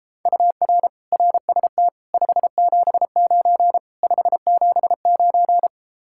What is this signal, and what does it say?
Morse code, heard as a keyed tone, about 33 wpm, 700 Hz. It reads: UR RST 579 579